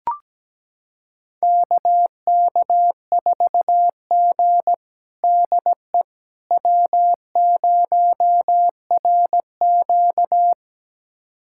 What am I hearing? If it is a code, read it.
Morse code, heard as a keyed tone, about 17 wpm, 700 Hz. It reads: KK4G DE W0RQ
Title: USA Callsign DE Callsign - 4 Characters - 17wpm